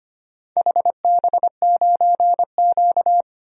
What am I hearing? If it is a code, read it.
Morse code, heard as a keyed tone, about 25 wpm, 700 Hz. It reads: HB9Q
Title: DXCC-100 Call Signs - 4 Characters - 25wpm